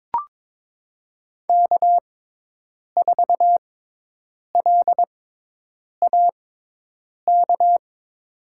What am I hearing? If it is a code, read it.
Morse code, heard as a keyed tone, about 21 wpm, spaced as for 8 wpm, 700 Hz. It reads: K4LAK
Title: USA Call Signs - 5 Characters - 6x Character Spacing - 22wpm